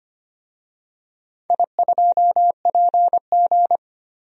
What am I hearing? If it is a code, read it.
Morse code, heard as a keyed tone, about 25 wpm, 700 Hz. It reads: I2PG
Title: DXCC-100 Call Signs - 4 Characters - 25wpm